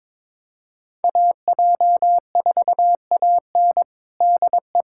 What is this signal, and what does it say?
Morse code, heard as a keyed tone, about 22 wpm, 700 Hz. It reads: AJ4AN DE
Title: USA Callsign DE Callsign - 5 Characters - 22wpm